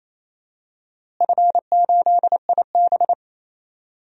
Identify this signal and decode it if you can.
Morse code, heard as a keyed tone, about 28 wpm, 700 Hz. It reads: F8IB
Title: DXCC-100 Call Signs - 4 Characters - Repeated 3x - 28wpm